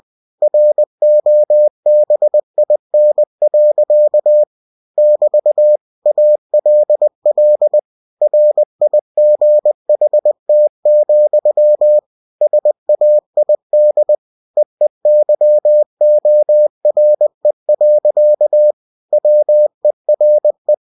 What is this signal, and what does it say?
Morse code, heard as a keyed tone, about 20 wpm, 600 Hz. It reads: ROBIN. <BT> ALL RIGHT, SAID EEYORE. WERE
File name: winnie-20wpm